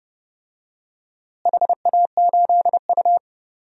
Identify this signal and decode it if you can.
Morse code, heard as a keyed tone, about 30 wpm, 700 Hz. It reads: HA8U
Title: DXCC-100 Call Signs - 4 Characters - Repeated 3x - 30wpm